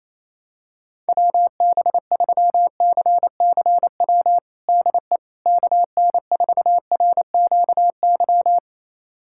Morse code, heard as a keyed tone, about 28 wpm, 700 Hz. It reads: WB3CCW DE KN4RQY
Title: USA Callsign DE Callsign - 6 Characters - 28wpm